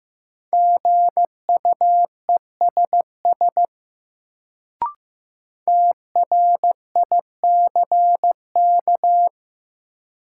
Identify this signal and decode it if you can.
Morse code, heard as a keyed tone, about 15 wpm, 700 Hz. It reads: GUESS TRICK
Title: Sentences from Top 1500 Words - New Word Drill - 15wpm